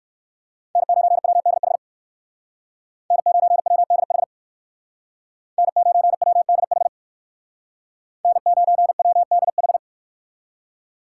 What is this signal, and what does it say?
Morse code, heard as a keyed tone, about 58 wpm, 700 Hz. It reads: N9WDH N9WDH N9WDH N9WDH